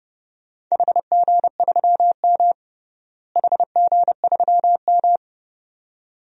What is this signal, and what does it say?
Morse code, heard as a keyed tone, about 32 wpm, 700 Hz. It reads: HG3M HG3M
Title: DXCC-100 Call Signs - 4 Characters - Repeated 2x - 30wpm